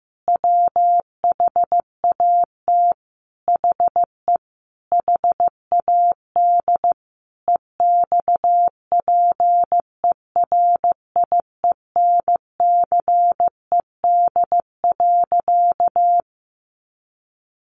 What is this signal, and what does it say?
Morse code, heard as a keyed tone, about 15 wpm, 700 Hz. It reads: WHAT HE HAD EXPERIENCED.
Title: Sentences from Top 1100 Words 15wpm